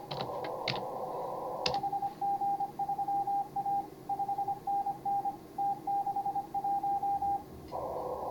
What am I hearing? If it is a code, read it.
Morse code, heard as a keyed tone, about 26 wpm, 810 Hz. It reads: HG3A 5NN T61
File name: CW / HA5MA RHOB HANG CW 2017